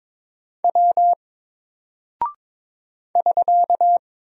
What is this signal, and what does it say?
Morse code, heard as a keyed tone, about 22 wpm, 700 Hz. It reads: W <SK>
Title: Single Character (Letter:Number:Punctuation:Prosign) 22wpm